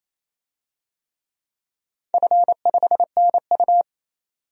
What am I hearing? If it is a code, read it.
Morse code, heard as a keyed tone, about 28 wpm, 700 Hz. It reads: F5NU
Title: DXCC-100 Call Signs - 4 Characters - 28wpm